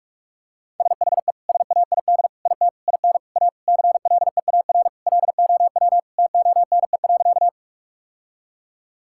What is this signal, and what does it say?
Morse code, heard as a keyed tone, about 46 wpm, 700 Hz. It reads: SHE SAID IT IN A CLEAR LOW TONE.
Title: Sentences from Top 1600 Words 45wpm